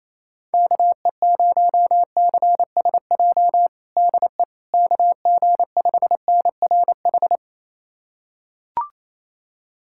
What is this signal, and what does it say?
Morse code, heard as a keyed tone, about 28 wpm, 700 Hz. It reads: KE0CSJ DE KG5NRH
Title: USA Callsign DE Callsign - 6 Characters - 28wpm